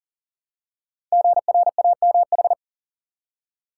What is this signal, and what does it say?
Morse code, heard as a keyed tone, about 38 wpm, 690 Hz. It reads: GRAMH
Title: Code Groups - 5 Letters - 40wpm